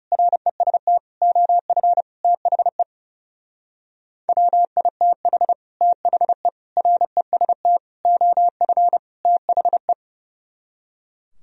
Morse code, heard as a keyed tone, about 36 wpm, 700 Hz. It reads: REST OF THE WITH THE REST OF THE
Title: Speed-Racing - Sets of 5 Words 30wpm